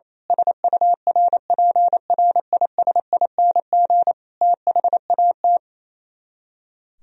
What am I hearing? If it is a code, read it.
Morse code, heard as a keyed tone, about 26 wpm, 700 Hz. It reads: SURPRISING THAT
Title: Speed-Racing - Sets of 5 Words 28wpm